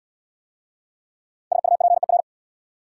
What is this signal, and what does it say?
Morse code, heard as a keyed tone, about 73 wpm, 700 Hz. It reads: SUPER